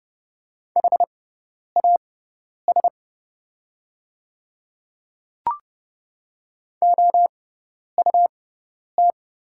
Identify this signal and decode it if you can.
Morse code, heard as a keyed tone, about 30 wpm, spaced as for 10 wpm, 700 Hz. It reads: HAS OUT
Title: 6x Character Spacing - Top 100 Words - 30wpm